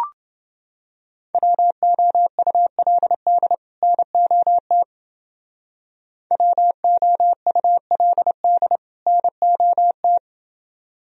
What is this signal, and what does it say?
Morse code, heard as a keyed tone, about 29 wpm, 700 Hz. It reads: WOULD NOT WOULD NOT